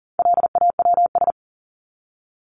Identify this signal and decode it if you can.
Morse code, heard as a keyed tone, about 40 wpm, 700 Hz. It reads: LAWS
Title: ES:S Suffix - Top 100 Words with Base Words 40wpm